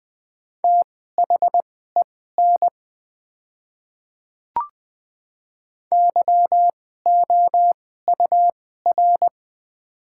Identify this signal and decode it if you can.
Morse code, heard as a keyed tone, about 20 wpm, spaced as for 15 wpm, 690 Hz. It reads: THEN YOUR
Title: 2x Character Spacing - Top 100 Words - 20wpm